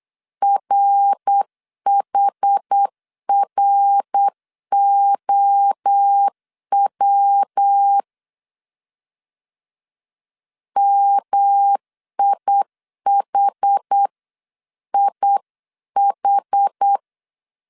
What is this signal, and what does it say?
Morse code, heard as a keyed tone, about 8 wpm, 800 Hz. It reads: RHROW MIHIH
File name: CW / DET / 1 3 1500C S C 8wpm 4214